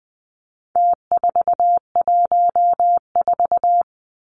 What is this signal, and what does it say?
Morse code, heard as a keyed tone, about 20 wpm, 700 Hz. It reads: T414
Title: Numbers - Contest Style - 4-Digit 20wpm